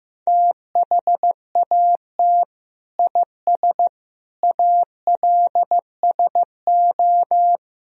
Morse code, heard as a keyed tone, about 15 wpm, 700 Hz. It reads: THAT IS ALSO
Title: Sentences from Top 1000 Words - Review Part 1 - 15wpm